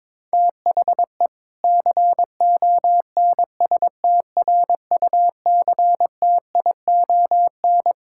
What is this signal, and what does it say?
Morse code, heard as a keyed tone, about 22 wpm, 700 Hz. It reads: THE CONSTRUCTION
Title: Sentences from Top 1600 Words 22wpm